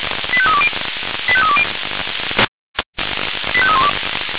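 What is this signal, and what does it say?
Five-tone selective call (ZVEI), heard as quick runs of five tones, tones of about 70 ms. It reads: 85320 85320 85312
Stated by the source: ZVEI five-tone selective call received on an SDR.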